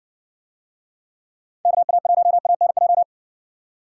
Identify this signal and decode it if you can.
Morse code, heard as a keyed tone, about 57 wpm, 700 Hz. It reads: KA1ANJ